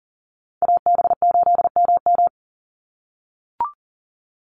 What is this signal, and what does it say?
Morse code, heard as a keyed tone, about 40 wpm, 700 Hz. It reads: AB8MM
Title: USA Call Signs 40wpm